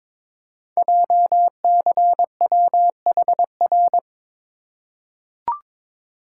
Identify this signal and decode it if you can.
Morse code, heard as a keyed tone, about 22 wpm, 700 Hz. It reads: JCWHR
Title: Code Groups - 5 Letters - 22wpm